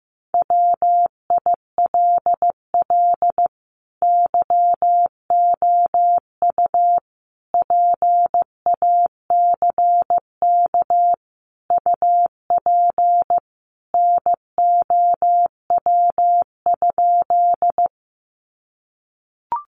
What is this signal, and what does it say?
Morse code, heard as a keyed tone, about 15 wpm, 700 Hz. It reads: WILL YOU PACK UP NOW?
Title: Sentences from Top 1300 Words 15wpm